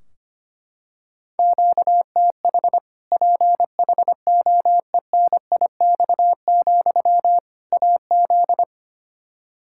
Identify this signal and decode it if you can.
Morse code, heard as a keyed tone, about 25 wpm, 700 Hz. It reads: QTH PHOENIX, AZ
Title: Practice QSO - Exchange 1 - Beginning - 25wpm